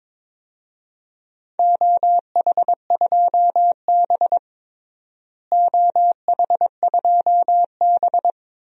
Morse code, heard as a keyed tone, about 22 wpm, 700 Hz. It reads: OH2B OH2B
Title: DXCC-100 Call Signs - 4 Characters - Repeated 3x - 22wpm